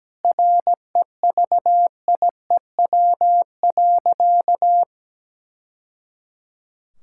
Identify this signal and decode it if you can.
Morse code, heard as a keyed tone, about 17 wpm, 700 Hz. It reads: REVIEW.
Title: Sentences from Top 1000 Words - Review Part 6 - 17wpm